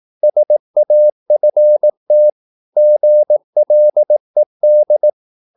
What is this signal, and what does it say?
Morse code, heard as a keyed tone, about 18 wpm, 600 Hz. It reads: SAFT GLED
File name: Korte ord DK 18wpm